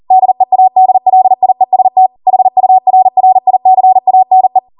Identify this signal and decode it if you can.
Morse code, heard as a keyed tone, about 40 wpm, 750 Hz. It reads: DEADLIEST HURRICANE